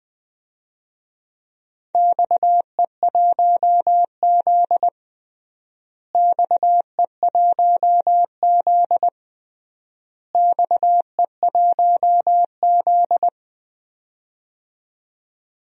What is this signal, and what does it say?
Morse code, heard as a keyed tone, about 20 wpm, 700 Hz. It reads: XE1Z XE1Z XE1Z
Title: DXCC-100 Call Signs - 4 Characters - Repeated 3x - 20wpm